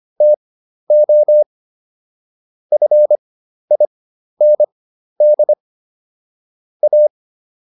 Morse code, heard as a keyed tone, about 25 wpm, spaced as for 12 wpm, 600 Hz. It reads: TO FIND A